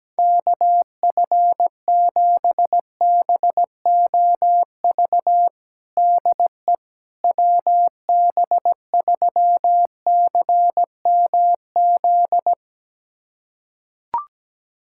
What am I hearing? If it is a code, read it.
Morse code, heard as a keyed tone, about 17 wpm, 700 Hz. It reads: KF7BOV DE WB3CMZ
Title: USA Callsign DE Callsign 17wpm